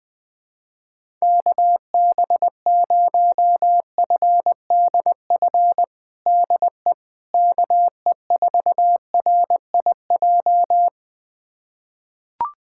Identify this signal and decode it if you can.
Morse code, heard as a keyed tone, about 21 wpm, 700 Hz. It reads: KB0FDF DE KE4RIJ
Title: USA Callsign DE Callsign 20wpm